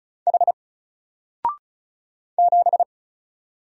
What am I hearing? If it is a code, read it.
Morse code, heard as a keyed tone, about 33 wpm, 700 Hz. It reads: H 7
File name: Single-Letter-Number-35wpm